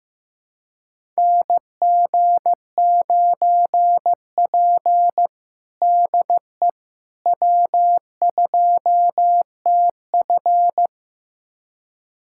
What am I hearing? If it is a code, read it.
Morse code, heard as a keyed tone, about 15 wpm, 700 Hz. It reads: NG9P DE W2TF
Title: USA Callsign DE Callsign - 4 Characters - 15wpm